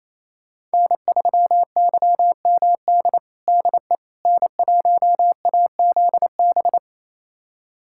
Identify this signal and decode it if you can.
Morse code, heard as a keyed tone, about 28 wpm, 700 Hz. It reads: N3YMD DE N1AZB
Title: USA Callsign DE Callsign - 5 Characters - 28wpm